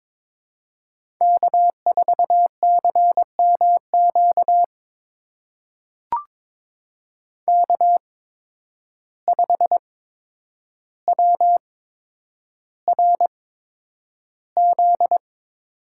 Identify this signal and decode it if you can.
Morse code, heard as a keyed tone, about 22 wpm, spaced as for 9 wpm, 700 Hz. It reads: K4CMQ K5WRZ
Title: USA Call Signs - 5 Characters - 8x Character Spacing - 22wpm